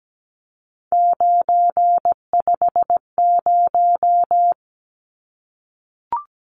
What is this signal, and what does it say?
Morse code, heard as a keyed tone, about 17 wpm, 700 Hz. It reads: 950
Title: Numbers - 3-Digit 17wpm